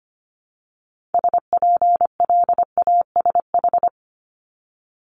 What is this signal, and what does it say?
Morse code, heard as a keyed tone, about 25 wpm, 700 Hz. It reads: SPLASH
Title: Multi-letter Phonemes Repeated 3x 25wpm